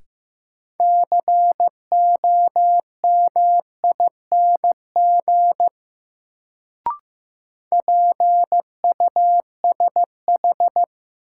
Morse code, heard as a keyed tone, about 15 wpm, 700 Hz. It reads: COMING PUSH
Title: Sentences from Top 1300 Words - New Word Drill - 15wpm